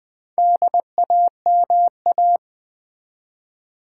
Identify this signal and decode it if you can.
Morse code, heard as a keyed tone, about 20 wpm, 700 Hz. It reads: DAMA
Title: Build A Word - 2 Letters At A Time - 8-Letter Words - 20wpm